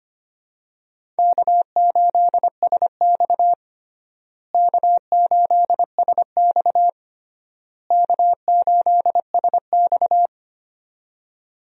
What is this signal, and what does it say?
Morse code, heard as a keyed tone, about 25 wpm, 700 Hz. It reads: K8SX K8SX K8SX
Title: USA Call Signs - 4 Characters - Repeated 3x - 25wpm